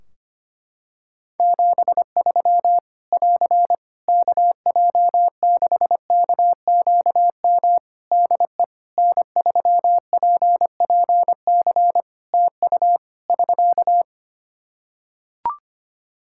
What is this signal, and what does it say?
Morse code, heard as a keyed tone, about 25 wpm, 700 Hz. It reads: 73 <AR> KJ6KQM DE N3PPC TU <SK>
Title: K1USN SST QSO Practice 25wpm